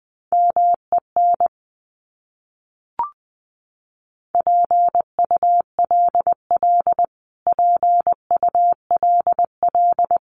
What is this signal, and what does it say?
Morse code, heard as a keyed tone, about 20 wpm, 700 Hz. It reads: MEN PULL PULL
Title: Top 500 Words Repeated 20wpm